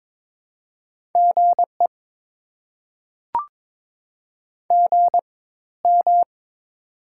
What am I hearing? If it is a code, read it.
Morse code, heard as a keyed tone, about 22 wpm, spaced as for 14 wpm, 690 Hz. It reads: GE GM
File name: Top 20 QSO Elements - 4x Character Spacing - 22wpm